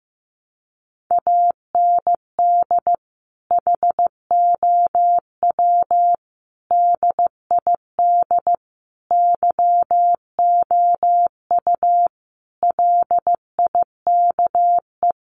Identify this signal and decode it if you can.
Morse code, heard as a keyed tone, about 15 wpm, 700 Hz. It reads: AND HOW DID YOU LIKE
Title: Sentences from Top 1200 Words 15wpm